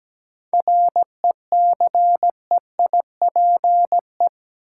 Morse code, heard as a keyed tone, about 17 wpm, 700 Hz. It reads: RECEIPE
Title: Sentences from Top 1200 Words - New Word Drill - 17wpm